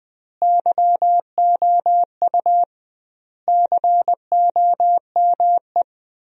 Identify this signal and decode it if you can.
Morse code, heard as a keyed tone, about 20 wpm, 700 Hz. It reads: YOU COME
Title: 2x Word-Spacing - Sentences from Top 100 Words - 20wpm